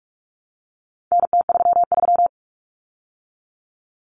Morse code, heard as a keyed tone, about 44 wpm, 700 Hz. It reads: NT33
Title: Common Cut Numbers - 4-Digit 45wpm